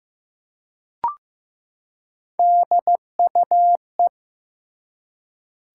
Morse code, heard as a keyed tone, about 15 wpm, 700 Hz. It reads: DUE
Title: Three Letter Words - 15wpm